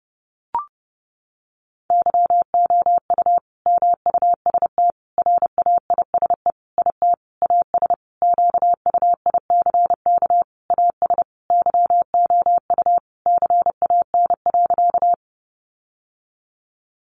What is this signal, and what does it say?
Morse code, heard as a keyed tone, about 28 wpm, 700 Hz. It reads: YOU MUST RAISE IT AS QUICK AS YOU CAN.
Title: Sentences from Top 1400 Words 30wpm